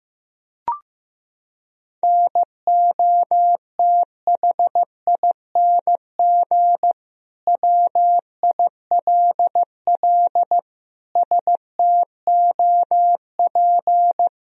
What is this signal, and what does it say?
Morse code, heard as a keyed tone, about 16 wpm, 700 Hz. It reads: NOTHING WILL STOP
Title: Sentences from Top 1000 Words - Review Part 1 - 15wpm